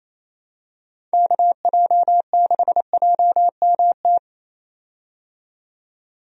Morse code, heard as a keyed tone, about 28 wpm, 700 Hz. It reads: KJ6JMT
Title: USA Call Signs - 6 Characters - 28wpm